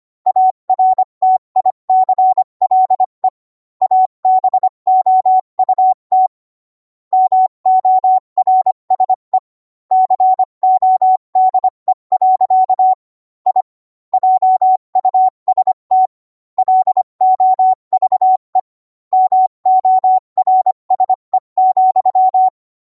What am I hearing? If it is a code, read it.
Morse code, heard as a keyed tone, about 25 wpm, 750 Hz. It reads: ARTICLE ABOUT MORSE CODE. I JUST LOVE MORSE,